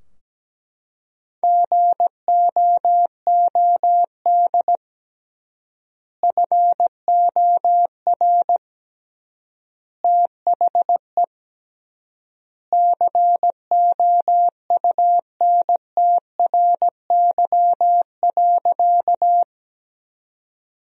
Morse code, heard as a keyed tone, about 17 wpm, 700 Hz. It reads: GOOD FOR THE COUNTRY.
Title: 3x Word-Spacing - Sentences from Top 300 Words - 17wpm